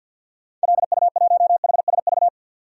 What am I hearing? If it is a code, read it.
Morse code, heard as a keyed tone, about 46 wpm, 700 Hz. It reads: LU1HSV
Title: DXCC-100 Call Signs - 6 Characters - 50wpm